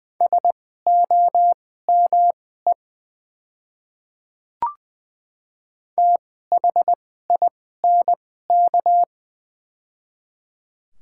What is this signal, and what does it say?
Morse code, heard as a keyed tone, about 20 wpm, spaced as for 14 wpm, 700 Hz. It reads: SOME THINK
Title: 2x Character Spacing - Top 200 Words - 20wpm